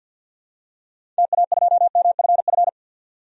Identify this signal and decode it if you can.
Morse code, heard as a keyed tone, about 52 wpm, 700 Hz. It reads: TA2MFF
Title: DXCC-100 Call Signs - 6 Characters - 50wpm